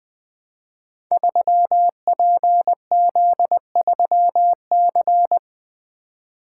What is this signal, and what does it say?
Morse code, heard as a keyed tone, about 20 wpm, 700 Hz. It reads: 3PZ3C
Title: Code Groups - 5 Letter:Numbers - 20wpm